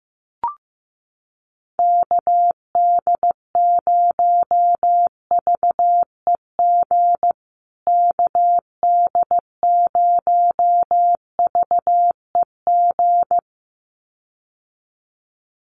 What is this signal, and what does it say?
Morse code, heard as a keyed tone, about 15 wpm, 700 Hz. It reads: KD0VEG KD0VEG
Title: USA Call Signs Repeated 2x 15wpm